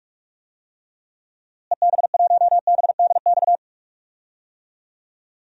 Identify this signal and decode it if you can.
Morse code, heard as a keyed tone, about 41 wpm, 700 Hz. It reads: EB1BDX
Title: DXCC-100 Call Signs - 6 Characters - Repeated 2x - 45wpm